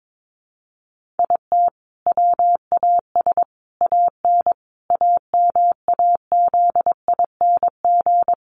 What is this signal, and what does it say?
Morse code, heard as a keyed tone, about 22 wpm, 700 Hz. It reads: IT WAS AN AMAZING
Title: Sentences from Top 1500 Words 22wpm